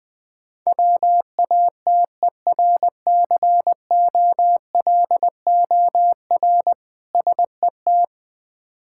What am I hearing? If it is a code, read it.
Morse code, heard as a keyed tone, about 21 wpm, 690 Hz. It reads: WATERCOLOR SET
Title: Difficult Words with Hints - 20wpm